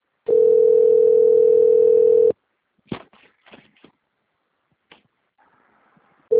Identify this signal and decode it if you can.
Ringback tone (North America), 440 + 480 Hz.